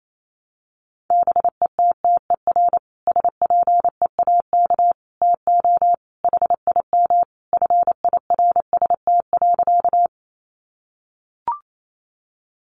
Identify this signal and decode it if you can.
Morse code, heard as a keyed tone, about 28 wpm, 700 Hz. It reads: BETTER SPEAK TO HIM FIRST.
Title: Sentences from Top 1100 Words 28wpm